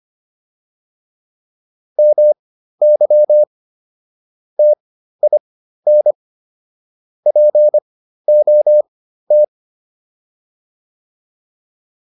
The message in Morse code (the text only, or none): MY TIN POT